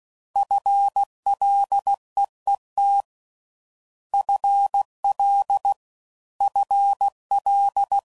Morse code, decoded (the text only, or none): FLEET FL FL